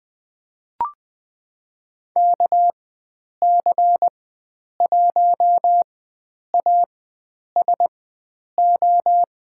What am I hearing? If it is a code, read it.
Morse code, heard as a keyed tone, about 20 wpm, spaced as for 9 wpm, 700 Hz. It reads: KC1ASO
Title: USA Call Signs - 6 Characters - 4x Character Spacing - 20wpm